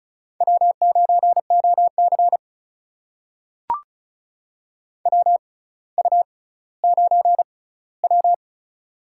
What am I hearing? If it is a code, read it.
Morse code, heard as a keyed tone, about 36 wpm, spaced as for 18 wpm, 700 Hz. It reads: W9OC WU9W
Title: USA Call Signs - 4 Characters - 6x Character Spacing - 35wpm